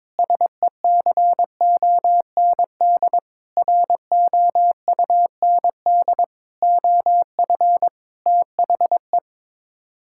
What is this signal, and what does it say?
Morse code, heard as a keyed tone, about 22 wpm, 700 Hz. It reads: SECOND ROUND OF THE
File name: practice / 22wpm / Sets of 6 Words - 22wpm